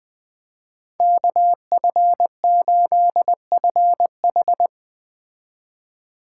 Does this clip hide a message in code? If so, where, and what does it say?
Yes: Morse code from 1.0 to 4.7 s, heard as a keyed tone, about 20 wpm, 700 Hz. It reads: KF8FH